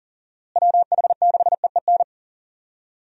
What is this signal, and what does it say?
Morse code, heard as a keyed tone, about 40 wpm, 700 Hz. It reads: WH6EEN
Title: USA Call Signs - 6 Characters - Repeated 2x - 40wpm